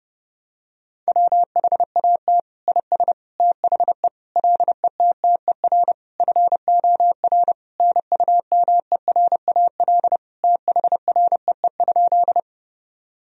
Morse code, heard as a keyed tone, about 30 wpm, 700 Hz. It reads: WHAT IS THE LETTER FOR NUMERAL THREE?
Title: Sentences from Top 1000 Words - Review Part 2 - 30wpm